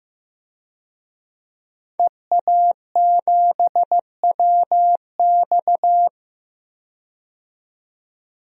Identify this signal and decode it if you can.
Morse code, heard as a keyed tone, about 15 wpm, 700 Hz. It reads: EA7WX
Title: DXCC-100 Call Signs - 5 Characters - 15wpm